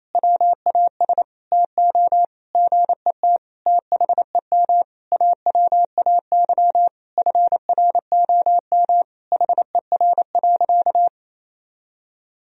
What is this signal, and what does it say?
Morse code, heard as a keyed tone, about 27 wpm, 700 Hz. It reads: WAS TO GET THEM AWAY FROM HER.
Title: Sentences from Top 1000 Words - Review Part 2 - 28wpm